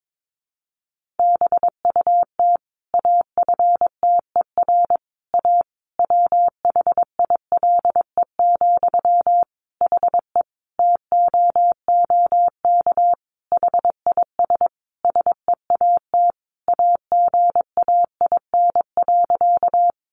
Morse code, heard as a keyed tone, about 22 wpm, 700 Hz. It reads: BUT AFTER A WHILE, HE TOOK HIS SEAT AGAIN.
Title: Sentences from Top 1500 Words 22wpm